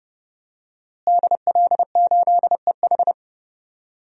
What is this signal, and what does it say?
Morse code, heard as a keyed tone, about 30 wpm, 700 Hz. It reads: DL8EH
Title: DXCC-100 Call Signs - 5 Characters - Repeated 2x - 30wpm